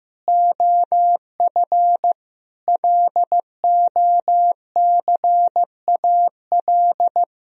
Morse code, heard as a keyed tone, about 15 wpm, 700 Hz. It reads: OF LOCAL
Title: Sentences from Top 1000 Words - Review Part 3 - 15wpm